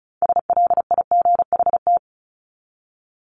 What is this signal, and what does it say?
Morse code, heard as a keyed tone, about 34 wpm, 700 Hz. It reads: SLIGHT
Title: LY Suffix - Top 100 Words with Base Words 35wpm